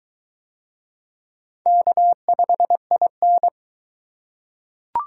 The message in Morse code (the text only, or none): K5IN